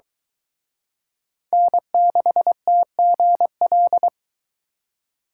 Morse code, heard as a keyed tone, about 23 wpm, 700 Hz. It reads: N6TGL